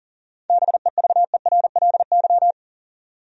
Morse code, heard as a keyed tone, about 38 wpm, 700 Hz. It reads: BEVERLY
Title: USA Common Names - 40wpm